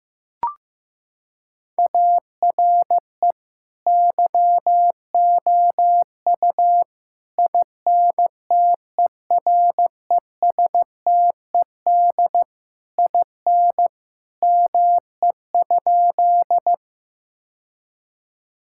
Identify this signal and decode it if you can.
Morse code, heard as a keyed tone, about 16 wpm, 700 Hz. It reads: ARE YOU INTERESTED IN ME?
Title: Sentences from Top 1000 Words - Review Part 6 - 15wpm